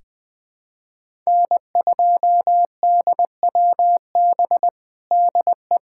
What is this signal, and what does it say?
Morse code, heard as a keyed tone, about 20 wpm, 700 Hz. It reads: N2DWB DE
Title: USA Callsign DE Callsign - 5 Characters - 20wpm